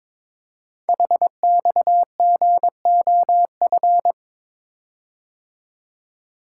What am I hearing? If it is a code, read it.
Morse code, heard as a keyed tone, about 22 wpm, 700 Hz. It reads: HXGOF
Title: Code Groups - 5 Letters - 22wpm